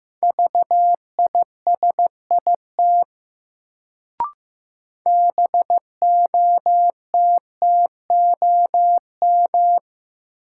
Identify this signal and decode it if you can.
Morse code, heard as a keyed tone, about 16 wpm, 700 Hz. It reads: VISIT BOTTOM
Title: Sentences from Top 900 Words - New Word Drill - 15wpm